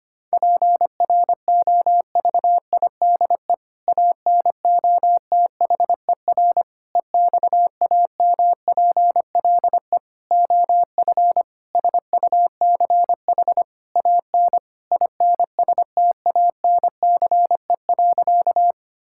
Morse code, heard as a keyed tone, about 25 wpm, 700 Hz. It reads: PROVIDE ANOTHER EXAMPLE OF SUCH AN INSTANCE.
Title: Sentences from Top 1000 Words - Review Part 5 - 25wpm